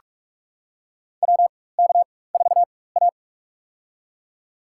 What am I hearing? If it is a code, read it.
Morse code, heard as a keyed tone, about 44 wpm, spaced as for 21 wpm, 700 Hz. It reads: WK4A
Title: USA Call Signs - 4x Character Spacing - 45wpm